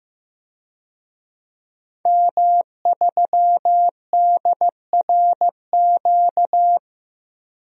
Morse code, heard as a keyed tone, about 15 wpm, 700 Hz. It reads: M3DRQ
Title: DXCC-100 Call Signs - 5 Characters - Repeated 2x - 15wpm